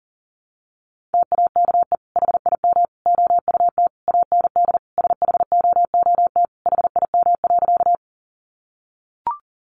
Morse code, heard as a keyed tone, about 40 wpm, 700 Hz. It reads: TAKE HIM OUT AND SHOOT HIM.
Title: Sentences from Top 1200 Words 40wpm